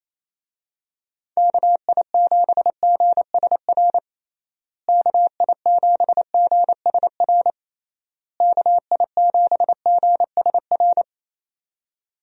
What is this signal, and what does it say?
Morse code, heard as a keyed tone, about 27 wpm, 700 Hz. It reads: KI7GSR KI7GSR KI7GSR
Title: USA Call Signs - 6 Characters - Repeated 3x - 28wpm